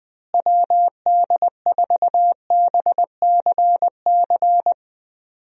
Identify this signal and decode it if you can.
Morse code, heard as a keyed tone, about 20 wpm, 700 Hz. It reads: WD4BCC
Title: USA Call Signs - 6 Characters - Repeated 3x - 20wpm